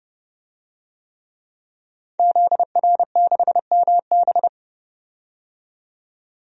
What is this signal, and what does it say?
Morse code, heard as a keyed tone, about 30 wpm, 700 Hz. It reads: ZR6MB
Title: DXCC-100 Call Signs - 5 Characters - 30wpm